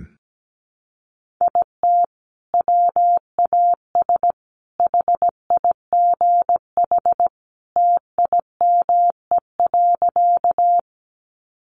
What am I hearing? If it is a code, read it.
Morse code, heard as a keyed tone, about 17 wpm, 700 Hz. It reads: IT WAS HIGH TIME.
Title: Sentences from Top 200 Words 17wpm